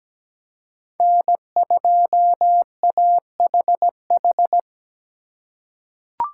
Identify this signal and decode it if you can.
Morse code, heard as a keyed tone, about 17 wpm, 700 Hz. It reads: N2AHH
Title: USA Call Signs - 5 Characters - 2x Character Spacing - 17wpm